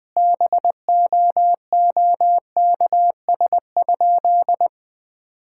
Morse code, heard as a keyed tone, about 20 wpm, 700 Hz. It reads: BOOKS?
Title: Sentences with 9-Word Phrases - 20wpm